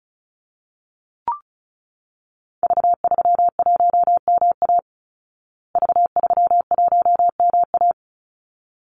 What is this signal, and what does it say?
Morse code, heard as a keyed tone, about 34 wpm, 700 Hz. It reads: V31MA V31MA
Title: Call Signs Repeated 3x - 35wpm